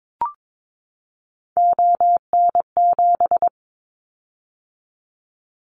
Morse code, heard as a keyed tone, about 22 wpm, 700 Hz. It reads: ON7
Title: Callsign Prefixes for Top 100 DXCC - 22wpm